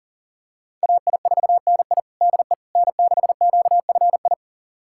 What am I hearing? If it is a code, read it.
Morse code, heard as a keyed tone, about 40 wpm, 700 Hz. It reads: AI4NI DE N6QFI
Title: USA Callsign DE Callsign - 5 Characters - 40wpm